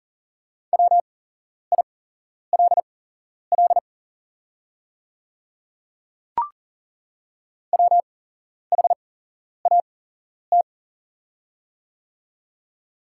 Morse code, heard as a keyed tone, about 40 wpm, spaced as for 11 wpm, 690 Hz. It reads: WILL WHAT
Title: 8x Character Spacing - Top 100 Words - 40wpm